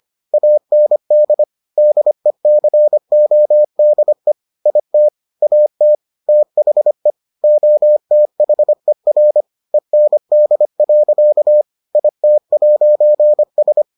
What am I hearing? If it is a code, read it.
Morse code, heard as a keyed tone, about 25 wpm, 600 Hz. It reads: AND DECODE IT AT THE OTHER END. IT'S